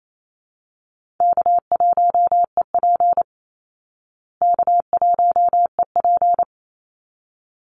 Morse code, heard as a keyed tone, about 28 wpm, 700 Hz. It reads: K1EP K1EP
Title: Call Signs Repeated 3x - 28wpm